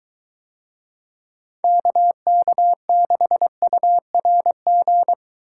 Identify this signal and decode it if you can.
Morse code, heard as a keyed tone, about 23 wpm, 700 Hz. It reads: KK6URG